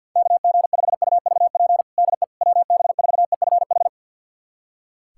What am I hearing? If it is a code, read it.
Morse code, heard as a keyed tone, about 51 wpm, 690 Hz. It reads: KG5UVP DE WB4EFH
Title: USA Callsign DE Callsign 50wpm